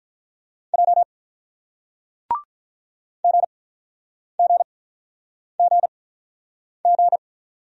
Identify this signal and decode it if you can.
Morse code, heard as a keyed tone, about 48 wpm, spaced as for 9 wpm, 690 Hz. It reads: J GGGG